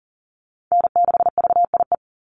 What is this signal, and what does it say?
Morse code, heard as a keyed tone, about 40 wpm, 700 Hz. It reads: N6VIE
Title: USA Call Signs 40wpm